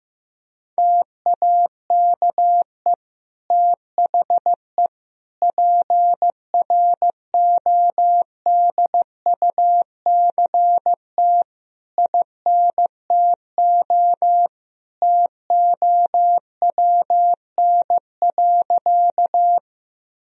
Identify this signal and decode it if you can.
Morse code, heard as a keyed tone, about 15 wpm, 700 Hz. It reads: TAKE THE PRODUCT INTO TOWN.
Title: Sentences from Top 1000 Words - Review Part 2 - 15wpm